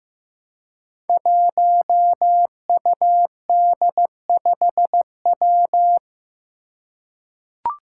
Morse code, heard as a keyed tone, about 15 wpm, 700 Hz. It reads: 1UD5W
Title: Five Challenging Letter-Number Combinations 15wpm